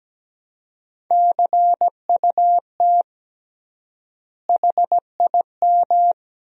Morse code, heard as a keyed tone, about 17 wpm, 700 Hz. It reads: CUT HIM
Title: 3x Word-Spacing - Sentences from Top 300 Words - 17wpm